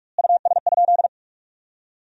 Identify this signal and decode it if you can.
Morse code, heard as a keyed tone, about 46 wpm, 700 Hz. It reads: US?
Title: 2x Word-Spacing - Sentences from Top 400 Words - 45wpm